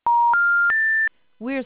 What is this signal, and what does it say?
Special information tone, with speech.